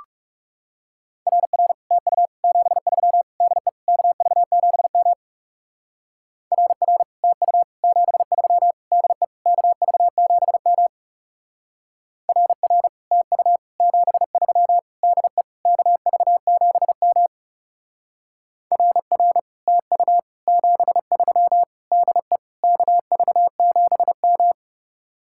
Morse code, heard as a keyed tone, about 36 wpm, 700 Hz. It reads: RR TU 73 DE KV7M RR TU 73 DE KV7M RR TU 73 DE KV7M RR TU 73 DE KV7M
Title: Speed-Racing Common QSO Phrases - 30wpm